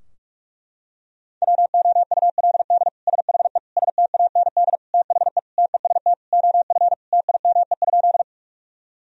Morse code, heard as a keyed tone, about 46 wpm, 700 Hz. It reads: WOULD SHE STAND THE TEST OF TIME?
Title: Sentences from Top 1000 Words - Review Part 3 - 45wpm